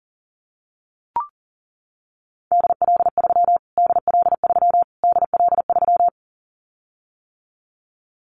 Morse code, heard as a keyed tone, about 37 wpm, 700 Hz. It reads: DL3 DL3 DL3
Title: Callsign Prefixes for Top 25 DXCC - Repeated 3x - 40wpm